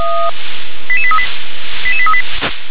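Five-tone selective call (ZVEI), heard as quick runs of five tones, tones of about 80 ms. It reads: 88838 88838
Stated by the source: ZVEI five-tone call received by SDR.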